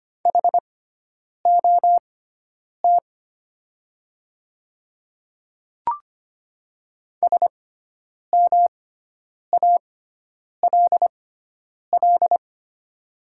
Morse code, heard as a keyed tone, about 25 wpm, spaced as for 9 wpm, 700 Hz. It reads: HOT SMALL
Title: 6x Character Spacing - Top 200 Words - 25wpm